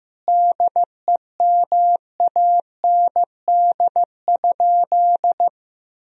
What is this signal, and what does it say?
Morse code, heard as a keyed tone, about 15 wpm, 700 Hz. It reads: DEMAND?
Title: Sentences from Top 1000 Words - Review Part 4 - 15wpm